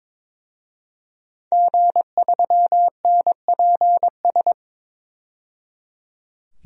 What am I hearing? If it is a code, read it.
Morse code, heard as a keyed tone, about 22 wpm, 700 Hz. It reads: G3NPS
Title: DXCC-100 Call Signs - 5 Characters - 22wpm